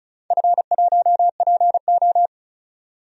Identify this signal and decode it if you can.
Morse code, heard as a keyed tone, about 36 wpm, 700 Hz. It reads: F1PO
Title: DXCC-100 Call Signs - 4 Characters - 35wpm